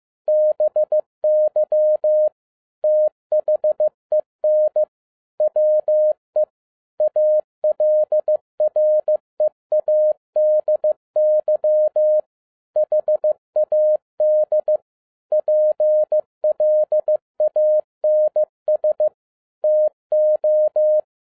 Morse code, heard as a keyed tone, about 15 wpm, 610 Hz. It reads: BY THEN WE ALREADY HAD PLANS TO
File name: 2011-07-mp3 / n07 A LOT HAS CHANGED 15wpm